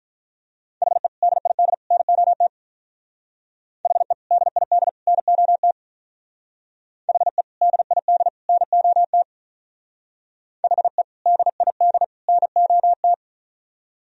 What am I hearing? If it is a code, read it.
Morse code, heard as a keyed tone, about 43 wpm, 700 Hz. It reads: HE DID NOT HE DID NOT HE DID NOT HE DID NOT
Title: Speed-Racing - Sets of 3 Words 35wpm